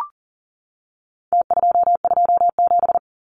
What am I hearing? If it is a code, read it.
Morse code, heard as a keyed tone, about 40 wpm, 700 Hz. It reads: T227
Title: Numbers - Contest Style - 4-Digit 40wpm